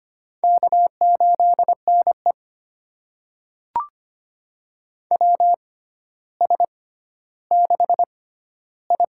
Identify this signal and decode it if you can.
Morse code, heard as a keyed tone, about 25 wpm, spaced as for 13 wpm, 700 Hz. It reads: K8NE WS6I
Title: USA Call Signs - 4 Characters - 6x Character Spacing - 25wpm